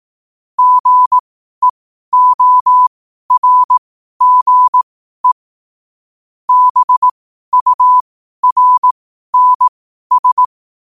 Morse code, heard as a keyed tone, about 18 wpm, spaced as for 12 wpm, 1000 Hz. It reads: GEORGE BURNS
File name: qotdfiles / 12wpm